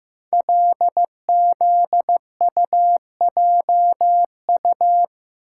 Morse code, heard as a keyed tone, about 15 wpm, 700 Hz. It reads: LZUJU
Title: Code Groups - 5 Letters - 15wpm